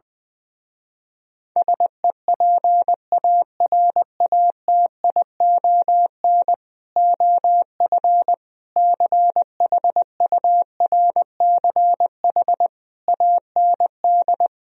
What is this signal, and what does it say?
Morse code, heard as a keyed tone, about 20 wpm, 700 Hz. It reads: SEPARATION OF CHURCH AND